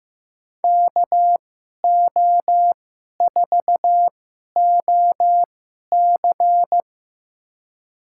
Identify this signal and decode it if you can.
Morse code, heard as a keyed tone, about 15 wpm, spaced as for 11 wpm, 700 Hz. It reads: KO4OC
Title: USA Call Signs - 5 Characters - 2x Character Spacing - 15wpm